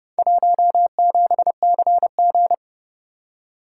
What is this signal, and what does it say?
Morse code, heard as a keyed tone, about 29 wpm, 700 Hz. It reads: 17CG